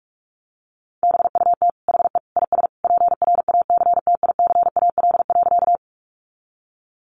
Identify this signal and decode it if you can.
Morse code, heard as a keyed tone, about 46 wpm, 700 Hz. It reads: BUT HE IS PRACTICAL.
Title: Sentences from Top 1400 Words 45wpm